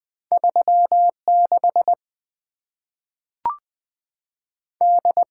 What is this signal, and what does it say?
Morse code, heard as a keyed tone, about 20 wpm, 700 Hz. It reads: 36 D
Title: Two Letter-Number Combinations - 20wpm